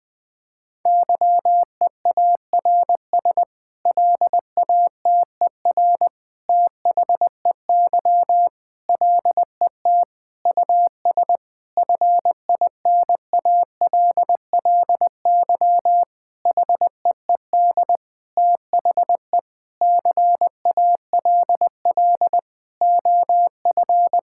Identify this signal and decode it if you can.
Morse code, heard as a keyed tone, about 20 wpm, 700 Hz. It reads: YEARS LATER THEY LET US FINALLY HEED THE CALL OF
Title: Sentences with 8-Word Phrases - Spoken Beginning - 20wpm